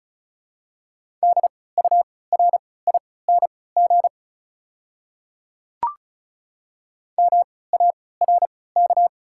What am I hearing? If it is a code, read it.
Morse code, heard as a keyed tone, about 36 wpm, spaced as for 20 wpm, 690 Hz. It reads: DURING MARK
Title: 3x Character Spacing - Top 500 Words - 35wpm